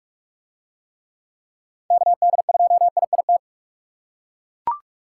Morse code, heard as a keyed tone, about 40 wpm, 700 Hz. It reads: KD2IIT